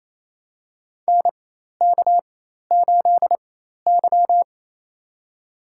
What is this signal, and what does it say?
Morse code, heard as a keyed tone, about 28 wpm, spaced as for 13 wpm, 700 Hz. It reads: NK8Y
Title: USA Call Signs - 4 Characters - 4x Character Spacing - 28wpm